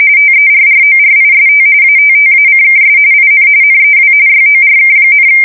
RTTY, about 45.45 baud. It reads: TERRORIST ATTACKS IN NEW YORK,
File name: fsk170 / USS-New-Jersey